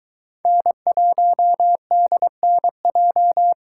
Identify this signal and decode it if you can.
Morse code, heard as a keyed tone, about 22 wpm, 700 Hz. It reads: N1DNJ